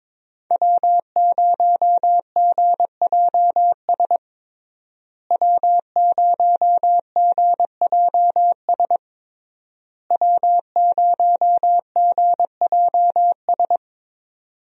Morse code, heard as a keyed tone, about 22 wpm, 700 Hz. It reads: W0GJS W0GJS W0GJS
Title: USA Call Signs - 5 Characters - Repeated 3x - 22wpm